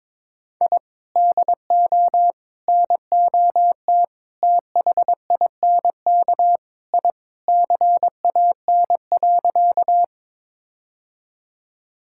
Morse code, heard as a keyed tone, about 22 wpm, 700 Hz. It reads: I DO NOT THINK I CAN.
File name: practice / 22wpm / Sentences with 4-Word Phrases - 22wpm